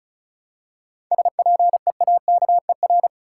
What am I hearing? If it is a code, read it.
Morse code, heard as a keyed tone, about 34 wpm, 700 Hz. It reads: SPEAKER
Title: Seven Letter Words - 35wpm